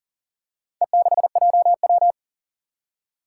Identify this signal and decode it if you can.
Morse code, heard as a keyed tone, about 42 wpm, 700 Hz. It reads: E6JW
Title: Four Challenging Letter-Number Combinations 40wpm